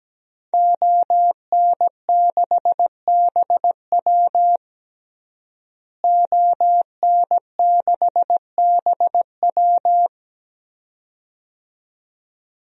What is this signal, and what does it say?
Morse code, heard as a keyed tone, about 17 wpm, 700 Hz. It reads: ON6BW ON6BW
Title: DXCC-100 Call Signs - 5 Characters - Repeated 2x - 17wpm